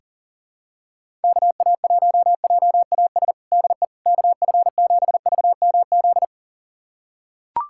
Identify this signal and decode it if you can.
Morse code, heard as a keyed tone, about 37 wpm, 700 Hz. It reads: KA1JAS DE KF7VMZ
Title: USA Callsign DE Callsign - 6 Characters - 40wpm